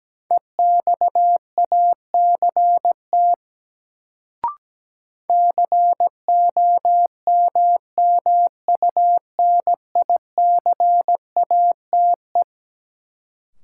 Morse code, heard as a keyed tone, about 17 wpm, 700 Hz. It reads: EXACT COMMUNICATE
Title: Sentences from Top 1300 Words - New Word Drill - 17wpm